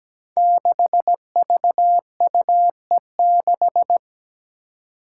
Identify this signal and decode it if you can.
Morse code, heard as a keyed tone, about 17 wpm, 700 Hz. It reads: 6VUE6
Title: Five Challenging Letter-Number Combinations 17wpm